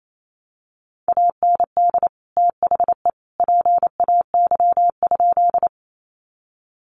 Morse code, heard as a keyed tone, about 28 wpm, 700 Hz. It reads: AND THE PAY?
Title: Sentences from Top 700 Words 28wpm